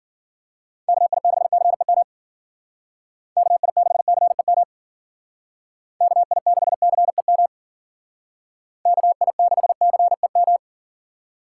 Morse code, heard as a keyed tone, about 49 wpm, 700 Hz. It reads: KI6CEM KI6CEM KI6CEM KI6CEM
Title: Speed Racing - USA 2x3 Call Signs 40wpm